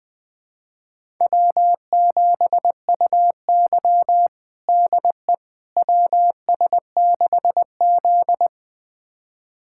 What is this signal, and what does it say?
Morse code, heard as a keyed tone, about 20 wpm, 700 Hz. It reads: W7UY DE WS6Z
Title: USA Callsign DE Callsign - 4 Characters - 20wpm